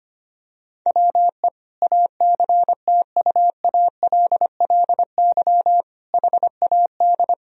Morse code, heard as a keyed tone, about 25 wpm, 700 Hz. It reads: WE ACTUALLY HAD
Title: Sentences from Top 1000 Words - Review Part 4 - 25wpm